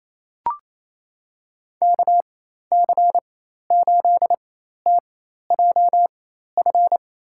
Morse code, heard as a keyed tone, about 28 wpm, spaced as for 13 wpm, 700 Hz. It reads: KC8TJF
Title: USA Call Signs - 6 Characters - 4x Character Spacing - 28wpm